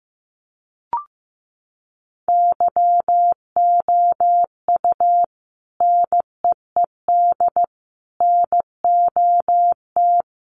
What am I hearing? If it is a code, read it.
Morse code, heard as a keyed tone, about 15 wpm, 700 Hz. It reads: YOU NEED NOT
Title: Sentences from Top 1400 Words 15wpm